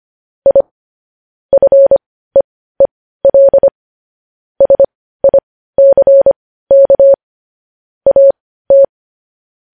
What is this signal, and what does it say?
Morse code, heard as a keyed tone, about 25 wpm, spaced as for 15 wpm, 550 Hz. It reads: I FEEL SICK AT
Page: news headlines in morse code at 15 wpm